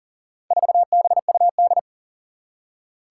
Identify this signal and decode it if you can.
Morse code, heard as a keyed tone, about 40 wpm, 700 Hz. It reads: 4BUD
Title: Four Challenging Letter-Number Combinations 40wpm